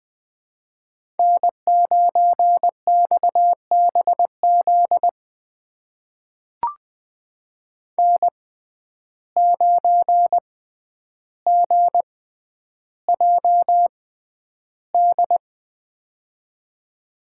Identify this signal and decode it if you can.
Morse code, heard as a keyed tone, about 20 wpm, spaced as for 10 wpm, 700 Hz. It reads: N9XBZ N9GJD
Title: USA Call Signs - 5 Characters - 6x Character Spacing - 20wpm